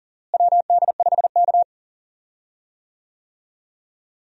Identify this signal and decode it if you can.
Morse code, heard as a keyed tone, about 40 wpm, 700 Hz. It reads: WD5K
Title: USA Call Signs - 4 Characters - Repeated 3x - 40wpm